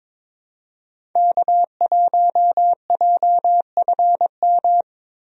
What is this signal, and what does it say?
Morse code, heard as a keyed tone, about 22 wpm, 700 Hz. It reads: K1JFM
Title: USA Call Signs - 5 Characters - Repeated 2x - 22wpm